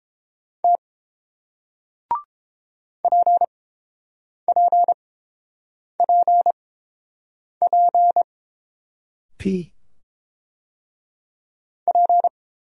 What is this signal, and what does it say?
Morse code, heard as a keyed tone, about 30 wpm, spaced as for 5 wpm, 700 Hz. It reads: T PPPPP